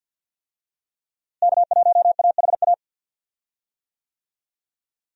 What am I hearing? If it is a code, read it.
Morse code, heard as a keyed tone, about 48 wpm, 700 Hz. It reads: K1AHA
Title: USA Call Signs - 5 Characters - Repeated 3x - 50wpm